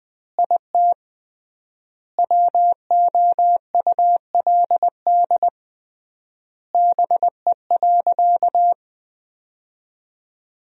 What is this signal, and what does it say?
Morse code, heard as a keyed tone, about 20 wpm, 700 Hz. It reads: IT WOULD BE.
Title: 3x Word-Spacing - Sentences from Top 200 Words - 20wpm